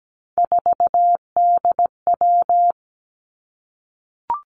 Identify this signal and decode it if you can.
Morse code, heard as a keyed tone, about 17 wpm, 700 Hz. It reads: 4DW
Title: Three Challenging Letter-Number Combinations 17wpm